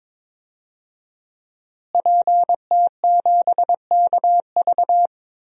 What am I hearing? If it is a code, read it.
Morse code, heard as a keyed tone, about 22 wpm, 700 Hz. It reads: PT7KV
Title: DXCC-100 Call Signs - 5 Characters - Repeated 3x - 22wpm